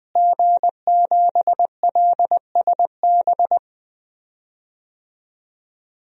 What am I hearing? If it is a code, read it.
Morse code, heard as a keyed tone, about 20 wpm, 700 Hz. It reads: G7LSB
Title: DXCC-100 Call Signs - 5 Characters - Repeated 2x - 20wpm